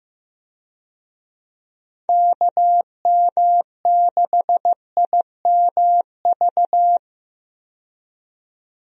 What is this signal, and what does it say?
Morse code, heard as a keyed tone, about 15 wpm, 700 Hz. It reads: KM6IMV
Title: USA Call Signs - 6 Characters - 15wpm